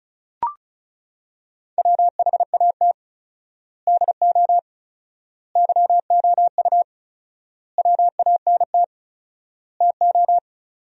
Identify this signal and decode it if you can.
Morse code, heard as a keyed tone, about 34 wpm, 700 Hz. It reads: WHAT DO YOU WANT TO
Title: 4x Word-Spacing - Sentences from Top 200 Words - 35wpm